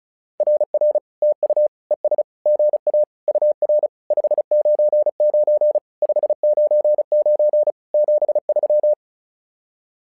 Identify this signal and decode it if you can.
Morse code, heard as a keyed tone, about 36 wpm, 600 Hz. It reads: RR TU ES GA UR 599 599 73
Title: SOTA QSO Practice 35wpm